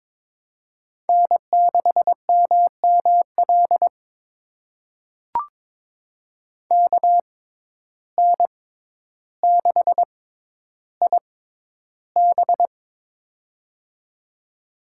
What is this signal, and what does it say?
Morse code, heard as a keyed tone, about 22 wpm, spaced as for 11 wpm, 700 Hz. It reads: N6MML KN6IB
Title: USA Call Signs - 5 Characters - 6x Character Spacing - 22wpm